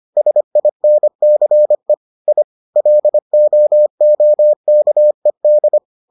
Morse code, heard as a keyed tone, about 25 wpm, 600 Hz. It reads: SINCE I LOOKED